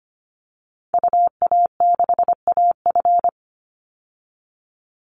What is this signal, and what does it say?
Morse code, heard as a keyed tone, about 25 wpm, 700 Hz. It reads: UA6AF
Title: Call Signs Repeated 2x - 25wpm